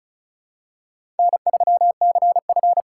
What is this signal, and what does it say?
Morse code, heard as a keyed tone, about 34 wpm, 700 Hz. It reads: N3CF
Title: USA Call Signs - 4 Characters - 4x Character Spacing - 35wpm